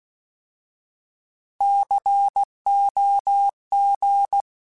Morse code, heard as a keyed tone, about 16 wpm, 800 Hz. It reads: COG
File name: Alphabetical / Alphabetical mp3s / Cs @15